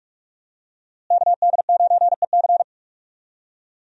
Morse code, heard as a keyed tone, about 46 wpm, 700 Hz. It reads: KD9EC
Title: USA Call Signs - 5 Characters - Repeated 2x - 45wpm